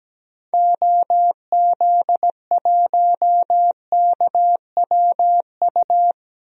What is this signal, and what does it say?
Morse code, heard as a keyed tone, about 17 wpm, 700 Hz. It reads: OZ1KWU
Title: DXCC-100 Call Signs - 6 Characters - 17wpm